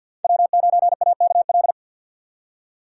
Morse code, heard as a keyed tone, about 51 wpm, 700 Hz. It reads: W9AKL
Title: USA Call Signs - 2x Character Spacing - 50wpm